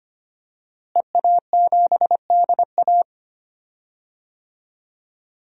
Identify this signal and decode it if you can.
Morse code, heard as a keyed tone, about 25 wpm, 700 Hz. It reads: EA7DA
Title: DXCC-100 Call Signs - 5 Characters - Repeated 3x - 25wpm